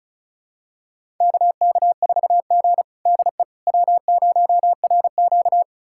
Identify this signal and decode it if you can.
Morse code, heard as a keyed tone, about 36 wpm, 700 Hz. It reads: KK4G DE W0RQ
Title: USA Callsign DE Callsign - 4 Characters - 35wpm